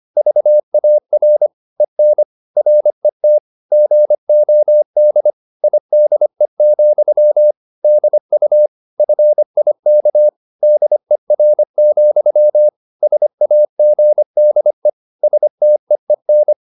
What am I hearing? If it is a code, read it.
Morse code, heard as a keyed tone, about 25 wpm, 600 Hz. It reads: VAR EN RET GOD IDE, DU FIK DER, SAGDE STEEN
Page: Ferie 25 wpm